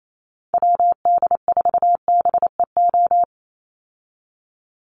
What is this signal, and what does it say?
Morse code, heard as a keyed tone, about 28 wpm, 700 Hz. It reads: WD4BEO
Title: USA Call Signs 28wpm